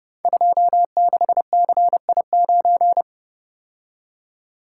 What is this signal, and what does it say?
Morse code, heard as a keyed tone, about 30 wpm, 700 Hz. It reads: 26CI9
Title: Code Groups - 5 Letter:Numbers - 30wpm